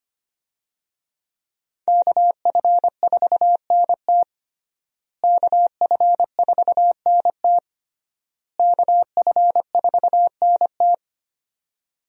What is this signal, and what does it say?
Morse code, heard as a keyed tone, about 25 wpm, 700 Hz. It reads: KF4NT KF4NT KF4NT
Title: USA Call Signs - 5 Characters - Repeated 3x - 25wpm